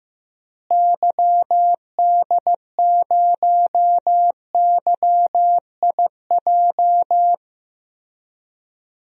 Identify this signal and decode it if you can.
Morse code, heard as a keyed tone, about 15 wpm, 700 Hz. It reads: YD0YIJ
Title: DXCC-100 Call Signs - 6 Characters - Repeated 3x - 15wpm